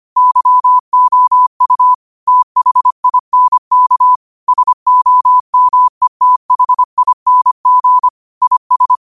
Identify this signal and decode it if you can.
Morse code, heard as a keyed tone, about 25 wpm, 1000 Hz. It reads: YOU THINK SOMETHING IS
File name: qotdfiles / 25wpm